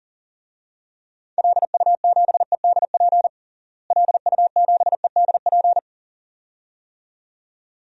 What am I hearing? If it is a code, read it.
Morse code, heard as a keyed tone, about 40 wpm, 700 Hz. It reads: LU7EDP LU7EDP
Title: DXCC-100 Call Signs - 6 Characters - Repeated 2x - 40wpm